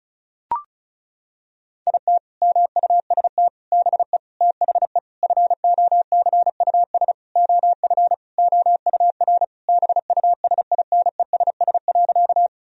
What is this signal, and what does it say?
Morse code, heard as a keyed tone, about 36 wpm, 700 Hz. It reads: IT MUST BE THE FOCUS OF OUR BUSINESS.
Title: Sentences from Top 1000 Words - Review Part 4 - 35wpm